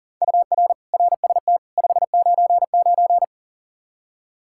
Morse code, heard as a keyed tone, about 40 wpm, 700 Hz. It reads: UR RST 599
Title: K1USN SST QSO Practice 40wpm